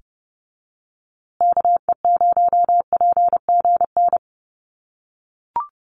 Morse code, heard as a keyed tone, about 30 wpm, 700 Hz. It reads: KE0PGN
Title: USA Call Signs Repeated 3x 30wpm